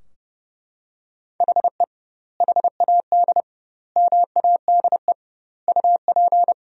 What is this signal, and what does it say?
Morse code, heard as a keyed tone, about 30 wpm, 700 Hz. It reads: HE HAD MADE UP
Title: 2x Word-Spacing - Sentences from Top 500 Words - 30wpm